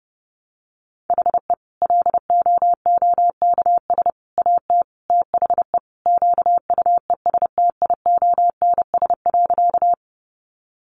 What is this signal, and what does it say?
Morse code, heard as a keyed tone, about 30 wpm, 700 Hz. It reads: HE LOOKS AT THE QUESTIONS.
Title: Additional Easy Phrases 30wpm